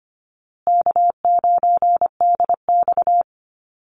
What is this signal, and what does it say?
Morse code, heard as a keyed tone, about 25 wpm, 700 Hz. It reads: K9DX
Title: Easy Call Signs Repeated 2x - 25wpm